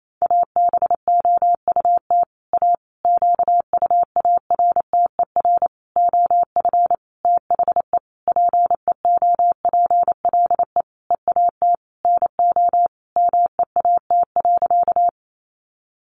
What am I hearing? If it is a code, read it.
Morse code, heard as a keyed tone, about 28 wpm, 700 Hz. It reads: ABOUT A QUARTER OF THE PEOPLE EAT NO MEAT.
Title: Sentences from Top 1300 Words 28wpm